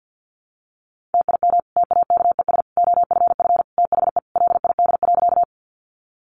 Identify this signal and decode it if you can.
Morse code, heard as a keyed tone, about 50 wpm, 700 Hz. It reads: TIM TAKES OFF THE LID.
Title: Easy Phrases 50wpm